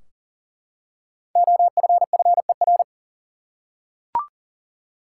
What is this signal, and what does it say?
Morse code, heard as a keyed tone, about 40 wpm, 700 Hz. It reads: OFFER
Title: Five Letter Words - 40wpm